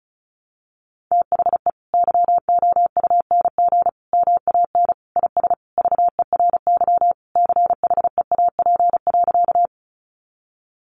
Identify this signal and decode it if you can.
Morse code, heard as a keyed tone, about 34 wpm, 700 Hz. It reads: THE YOUNG MAN IS VERY CHEAP.
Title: Sentences from Top 1200 Words 35wpm